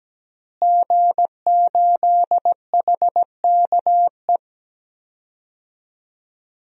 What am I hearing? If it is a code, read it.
Morse code, heard as a keyed tone, about 17 wpm, 700 Hz. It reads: G8HKE
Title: DXCC-100 Call Signs - 5 Characters - Repeated 3x - 17wpm